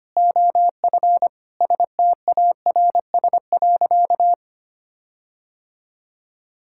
Morse code, heard as a keyed tone, about 25 wpm, 700 Hz. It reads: OF STARS.
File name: practice / 25wpm / Sentences with 4-Word Phrases - 25wpm